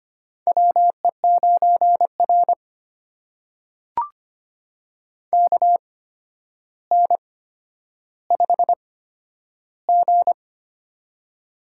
Morse code, heard as a keyed tone, about 25 wpm, spaced as for 11 wpm, 700 Hz. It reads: WE9R KN5G
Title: USA Call Signs - 4 Characters - 8x Character Spacing - 25wpm